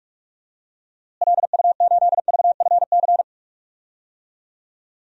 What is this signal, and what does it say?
Morse code, heard as a keyed tone, about 46 wpm, 700 Hz. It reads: LU8VFC
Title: DXCC-100 Call Signs - 6 Characters - Repeated 3x - 45wpm